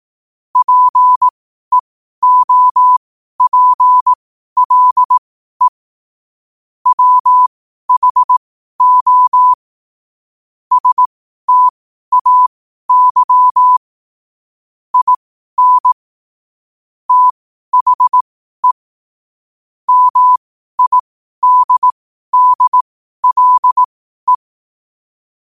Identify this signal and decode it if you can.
Morse code, heard as a keyed tone, about 18 wpm, spaced as for 12 wpm, 1000 Hz. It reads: PEOPLE WHO STAY IN THE MIDDLE
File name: qotdfiles / 12wpm